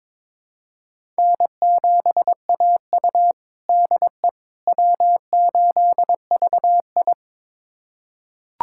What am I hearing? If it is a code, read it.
Morse code, heard as a keyed tone, about 22 wpm, 700 Hz. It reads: N7AU DE W8VI
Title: USA Callsign DE Callsign - 4 Characters - 22wpm